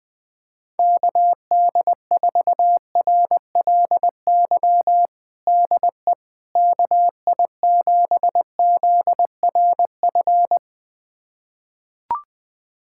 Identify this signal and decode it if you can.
Morse code, heard as a keyed tone, about 20 wpm, 700 Hz. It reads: KD4RLY DE KI7ZRF
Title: USA Callsign DE Callsign 20wpm